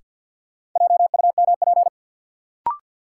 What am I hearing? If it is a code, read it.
Morse code, heard as a keyed tone, about 51 wpm, 700 Hz. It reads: JUMP
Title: Sentences from Top 1300 Words - New Word Drill Alt - 50wpm